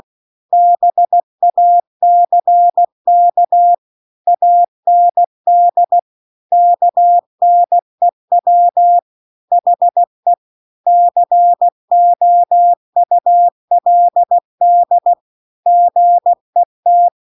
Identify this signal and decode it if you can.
Morse code, heard as a keyed tone, about 16 wpm, 700 Hz. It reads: BACK AND KNEW HE COULD GET
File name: CW-The-Big-Twoheaded-River-2-16WPM